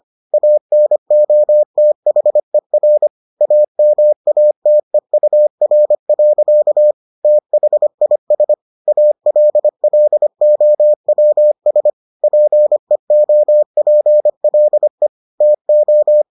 Morse code, heard as a keyed tone, about 25 wpm, 600 Hz. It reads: ANOTHER AMATEUR. THIS ALLOWS PEOPLE TO